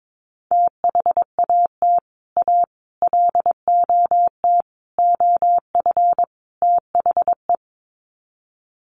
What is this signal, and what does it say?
Morse code, heard as a keyed tone, about 22 wpm, 700 Hz. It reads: THAT A LOT OF THE
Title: Sets of 5 Words Encore2 22wpm